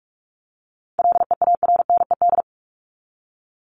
Morse code, heard as a keyed tone, about 44 wpm, 700 Hz. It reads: LEARNED
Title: ED Suffix - Top 100 Words with Base Words 45wpm